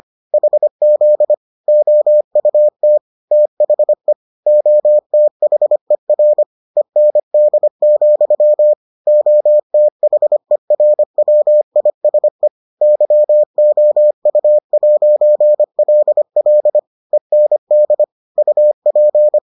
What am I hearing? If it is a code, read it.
Morse code, heard as a keyed tone, about 25 wpm, 600 Hz. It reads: HZ OUT THE OTHER END, OTHERWISE YOU'LL END UP